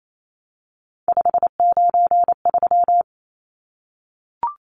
Morse code, heard as a keyed tone, about 28 wpm, 700 Hz. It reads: 593
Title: Numbers - 3-Digit 28wpm